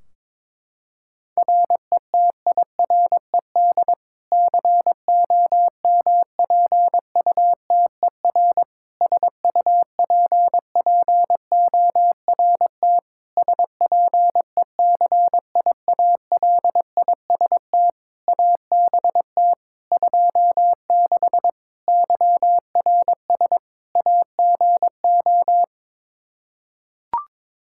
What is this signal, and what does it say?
Morse code, heard as a keyed tone, about 22 wpm, 700 Hz. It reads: RETIRED COMPUTER SUPPORT SPECIALIST ABT 26 YRS AGO
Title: Practice QSO - Exchange 3 - More Info - 22wpm